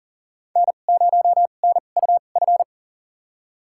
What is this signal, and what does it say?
Morse code, heard as a keyed tone, about 40 wpm, spaced as for 29 wpm, 700 Hz. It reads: N0NUF
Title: USA Call Signs - 5 Characters - 2x Character Spacing - 40wpm